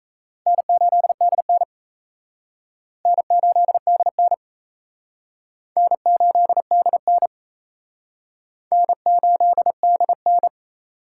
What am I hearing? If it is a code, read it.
Morse code, heard as a keyed tone, about 37 wpm, 700 Hz. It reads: N8DN N8DN N8DN N8DN